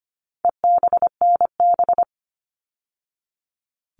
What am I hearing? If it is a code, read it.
Morse code, heard as a keyed tone, about 25 wpm, 700 Hz. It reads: EBNB
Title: All Cut Numbers - 4-Digit 25wpm